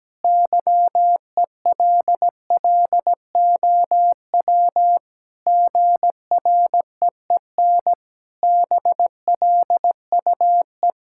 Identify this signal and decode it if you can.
Morse code, heard as a keyed tone, about 17 wpm, 700 Hz. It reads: YELLOW GREEN BLUE
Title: Pangram Challenge v2 - 17wpm